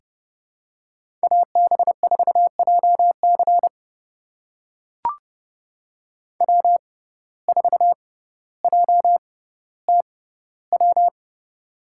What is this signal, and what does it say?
Morse code, heard as a keyed tone, about 32 wpm, spaced as for 16 wpm, 690 Hz. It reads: AB4JC W4JTW
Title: USA Call Signs - 5 Characters - 6x Character Spacing - 30wpm